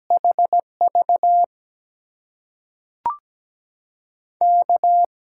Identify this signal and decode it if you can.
Morse code, heard as a keyed tone, about 17 wpm, spaced as for 5 wpm, 700 Hz. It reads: HVK
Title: USA Call Signs - 6 Characters - 8x Character Spacing - 17wpm